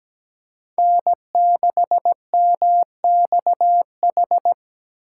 Morse code, heard as a keyed tone, about 20 wpm, 700 Hz. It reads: N6MXH
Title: Speed Racing - USA 1x3 Call Signs 17wpm